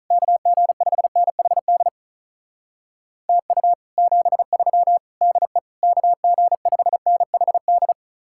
Morse code, heard as a keyed tone, about 43 wpm, 700 Hz. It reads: KG5NHD TU 73 DE KG5NHD
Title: Speed-Racing Common QSO Phrases - 35wpm